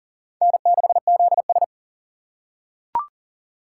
Morse code, heard as a keyed tone, about 40 wpm, 700 Hz. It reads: N6ZS
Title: USA Call Signs - 4 Characters - 6x Character Spacing - 40wpm